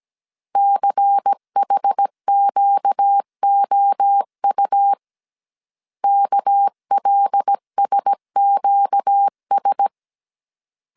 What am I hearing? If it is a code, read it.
Morse code, heard as a keyed tone, about 17 wpm, 800 Hz. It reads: CHQOU KLSQS
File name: CW / DET / 1 6 1500C S A 16wpm 2635